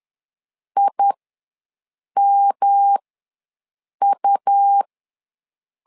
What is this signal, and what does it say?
Morse code, heard as a keyed tone, about 10 wpm, spaced as for 6 wpm, 800 Hz. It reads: IMU
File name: LEZIONI / 3 MWRO / 3 MW ws 10wpm 2213